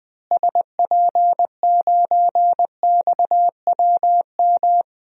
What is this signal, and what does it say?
Morse code, heard as a keyed tone, about 20 wpm, 700 Hz. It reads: SP9XWM
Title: DXCC-100 Call Signs - 6 Characters - Repeated 3x - 20wpm